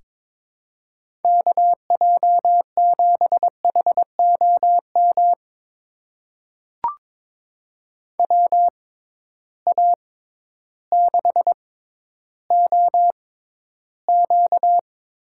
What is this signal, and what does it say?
Morse code, heard as a keyed tone, about 22 wpm, spaced as for 12 wpm, 700 Hz. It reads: KJ7HOM WA6OQ
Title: USA Call Signs - 6x Character Spacing - 22wpm